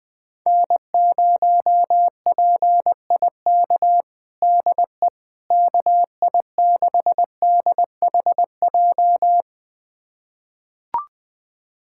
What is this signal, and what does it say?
Morse code, heard as a keyed tone, about 21 wpm, 700 Hz. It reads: N0PIK DE KI6DHJ
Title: USA Callsign DE Callsign 20wpm